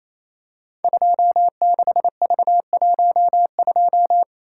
Morse code, heard as a keyed tone, about 28 wpm, 700 Hz. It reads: 26V12
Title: Code Groups - 5 Letter:Numbers - 28wpm